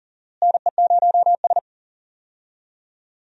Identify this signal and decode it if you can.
Morse code, heard as a keyed tone, about 39 wpm, 700 Hz. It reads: NE0S
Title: USA Call Signs - 4 Characters - Repeated 2x - 40wpm